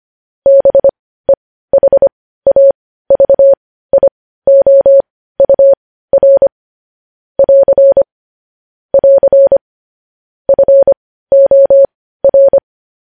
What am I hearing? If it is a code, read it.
Morse code, heard as a keyed tone, about 25 wpm, spaced as for 15 wpm, 550 Hz. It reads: BEHAVIOUR <AR> <AR> FOR
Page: news headlines in morse code at 15 wpm